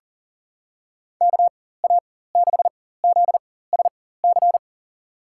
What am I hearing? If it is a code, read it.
Morse code, heard as a keyed tone, about 40 wpm, spaced as for 19 wpm, 700 Hz. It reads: KA6ZSC
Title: USA Call Signs - 4x Character Spacing - 40wpm